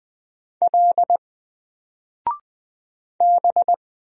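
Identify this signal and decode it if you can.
Morse code, heard as a keyed tone, about 20 wpm, 700 Hz. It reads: L B
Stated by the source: Single letter and number Morse drill at 20 wpm, from N4LCD.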